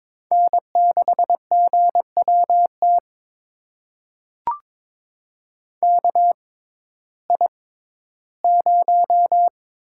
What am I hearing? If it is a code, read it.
Morse code, heard as a keyed tone, about 22 wpm, spaced as for 14 wpm, 700 Hz. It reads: N6GWT KI0
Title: USA Call Signs - 5 Characters - 6x Character Spacing - 22wpm